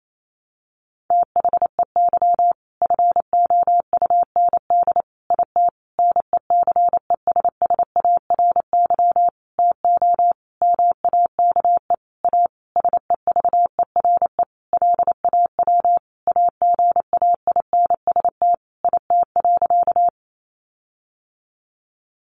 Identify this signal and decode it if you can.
Morse code, heard as a keyed tone, about 28 wpm, 700 Hz. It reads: THEY FOUND IT NECESSARY TO MAKE A SEVERE LAW AGAINST IT.
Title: Sentences from Top 1400 Words 28wpm